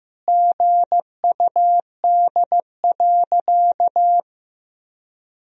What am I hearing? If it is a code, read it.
Morse code, heard as a keyed tone, about 15 wpm, 700 Hz. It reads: GUD.
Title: Sentences from Top 200 Words and Abbreviations 15wpm